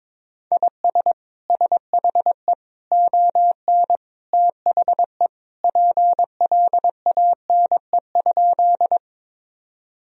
Morse code, heard as a keyed tone, about 22 wpm, 700 Hz. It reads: IS SHE ON THE PLANE?
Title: Sentences from Top 1000 Words - Review Part 3 - 22wpm